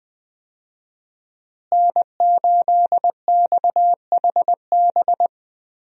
Morse code, heard as a keyed tone, about 21 wpm, 700 Hz. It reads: N8XHB
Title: USA Call Signs - 5 Characters - Repeated 2x - 20wpm